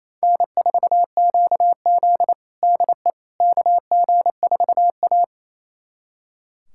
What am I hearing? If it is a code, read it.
Morse code, heard as a keyed tone, about 28 wpm, 700 Hz. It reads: N4QZ DE KG4A
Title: USA Callsign DE Callsign - 4 Characters - 28wpm